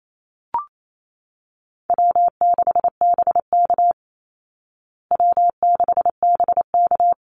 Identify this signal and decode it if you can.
Morse code, heard as a keyed tone, about 28 wpm, 700 Hz. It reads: W6BK W6BK
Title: Call Signs Repeated 3x - 28wpm